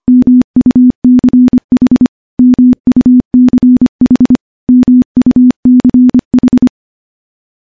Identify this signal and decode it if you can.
Morse code, heard as a keyed tone, about 25 wpm, 250 Hz. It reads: MUCH MUCH MUCH
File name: morse / morse-training-25WPM